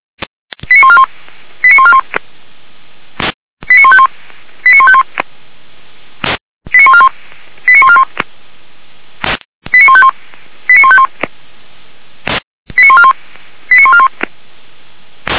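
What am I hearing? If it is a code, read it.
Five-tone selective call (ZVEI), heard as quick runs of five tones, tones of about 70 ms. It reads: 89141 89141 89151 89151 89141 89141 89151 89151 89141 89141
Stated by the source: ZVEI five-tone call received by SDR.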